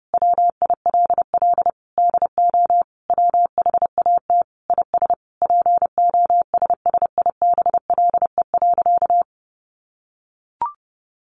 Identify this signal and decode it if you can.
Morse code, heard as a keyed tone, about 30 wpm, 700 Hz. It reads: WILL DO WHAT IS POSSIBLE.
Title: Sentences from Top 900 Words 30wpm